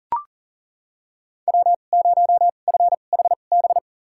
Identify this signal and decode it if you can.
Morse code, heard as a keyed tone, about 40 wpm, spaced as for 29 wpm, 700 Hz. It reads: W0FHB
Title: USA Call Signs - 2x Character Spacing - 40wpm